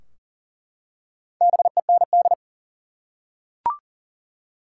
Morse code, heard as a keyed tone, about 40 wpm, 700 Hz. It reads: BEND
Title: Four Letter Words - 40wpm